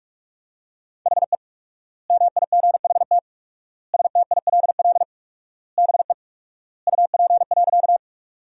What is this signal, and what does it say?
Morse code, heard as a keyed tone, about 46 wpm, 700 Hz. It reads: HE MIGHT STILL BE UP.
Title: 4x Word-Spacing - Sentences from Top 300 Words - 45wpm